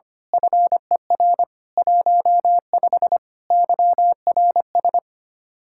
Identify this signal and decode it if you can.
Morse code, heard as a keyed tone, about 25 wpm, 700 Hz. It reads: FER 15 YRS
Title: Speed-Racing Common QSO Phrases - 25wpm